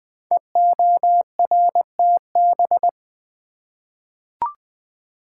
Morse code, heard as a keyed tone, about 20 wpm, 700 Hz. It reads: EORTB
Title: Code Groups - 5 Letter:Numbers - 20wpm